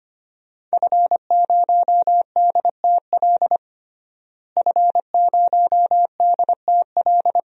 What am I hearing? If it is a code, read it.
Morse code, heard as a keyed tone, about 25 wpm, 700 Hz. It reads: F0DTL F0DTL
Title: DXCC-100 Call Signs - 5 Characters - Repeated 3x - 25wpm